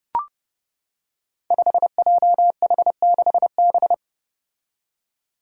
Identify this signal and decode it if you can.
Morse code, heard as a keyed tone, about 32 wpm, 700 Hz. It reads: 5JH6B
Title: Five Challenging Letter-Number Combinations 30wpm